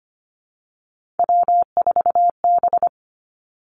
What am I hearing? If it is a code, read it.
Morse code, heard as a keyed tone, about 25 wpm, 700 Hz. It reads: W4B
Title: Three Challenging Letter-Number Combinations 25wpm